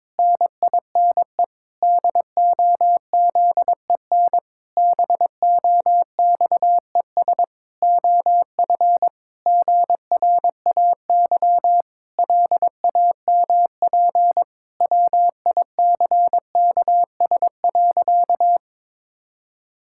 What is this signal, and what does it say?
Morse code, heard as a keyed tone, about 22 wpm, 700 Hz. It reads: NINE DOZEN BOXES OF GRAY LAMP WICKS.
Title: Pangram Challenge v2 - 22wpm